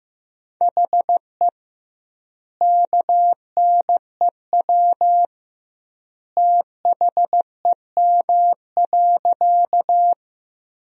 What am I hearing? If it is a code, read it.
Morse code, heard as a keyed tone, about 16 wpm, 700 Hz. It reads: HE KNEW THEM.
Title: 2x Word-Spacing - Sentences from Top 400 Words - 15wpm